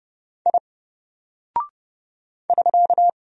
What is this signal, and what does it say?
Morse code, heard as a keyed tone, about 30 wpm, 700 Hz. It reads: I <SK>
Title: Single Character (Letter:Number:Punctuation:Prosign) Alt 30wpm